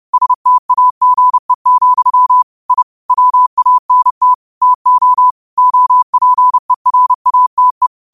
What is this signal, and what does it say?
Morse code, heard as a keyed tone, about 30 wpm, 1000 Hz. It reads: STAGE, I WANT TO OPERATE
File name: qotdfiles / 30wpm